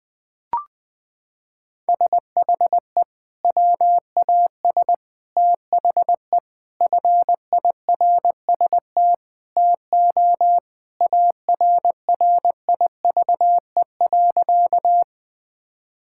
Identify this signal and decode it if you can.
Morse code, heard as a keyed tone, about 21 wpm, 700 Hz. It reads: SHE WAS THE FIRST TO ARRIVE.
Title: Sentences with 3-Word Phrases - 20wpm